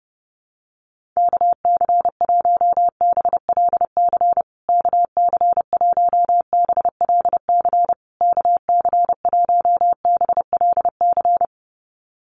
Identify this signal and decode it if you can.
Morse code, heard as a keyed tone, about 30 wpm, 700 Hz. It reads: KC1BLC KC1BLC KC1BLC
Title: USA Call Signs Repeated 3x 30wpm